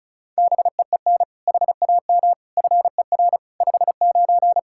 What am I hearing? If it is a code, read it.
Morse code, heard as a keyed tone, about 34 wpm, 700 Hz. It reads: BEEN HAM FER 59
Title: Practice QSO - Exchange 3 - More Info - 35wpm